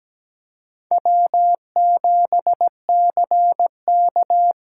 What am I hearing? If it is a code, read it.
Morse code, heard as a keyed tone, about 17 wpm, 700 Hz. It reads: W7CK
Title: USA Call Signs - 5 Characters - 8x Character Spacing - 17wpm